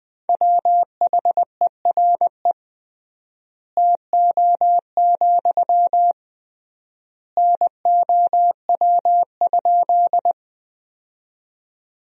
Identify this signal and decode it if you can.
Morse code, heard as a keyed tone, about 21 wpm, 700 Hz. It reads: WHERE TO, NOW?
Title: 3x Word-Spacing - Sentences from Top 200 Words - 20wpm